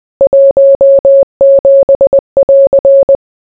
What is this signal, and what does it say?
Morse code, heard as a keyed tone, about 20 wpm, 550 Hz. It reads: 17<AR>